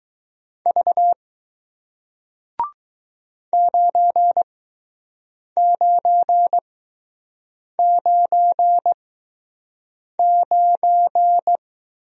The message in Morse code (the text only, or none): V 9999